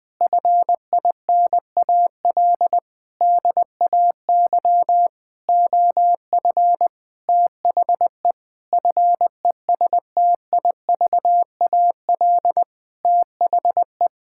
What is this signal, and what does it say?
Morse code, heard as a keyed tone, about 20 wpm, 700 Hz. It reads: FINAL DAY OF THE FESTIVAL THE
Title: Sentences with 6-Word Phrases - 20wpm